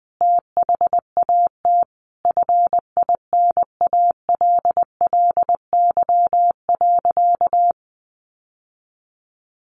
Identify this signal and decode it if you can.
Morse code, heard as a keyed tone, about 20 wpm, 700 Hz. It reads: THAT FINALLY.
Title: Sentences from Top 1000 Words 20wpm